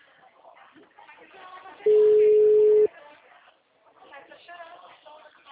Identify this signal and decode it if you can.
Ringback tone (Europe, 425 Hz), with speech.
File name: out- -2003- - - .47265